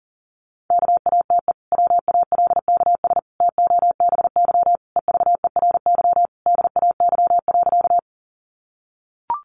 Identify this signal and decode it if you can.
Morse code, heard as a keyed tone, about 40 wpm, 700 Hz. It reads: KATE WALKS TOBY EVERY DAY.
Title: Easy Phrases 40wpm